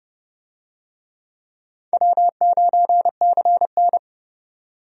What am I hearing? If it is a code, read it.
Morse code, heard as a keyed tone, about 32 wpm, 700 Hz. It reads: W9CN
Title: USA Call Signs - 4 Characters - 30wpm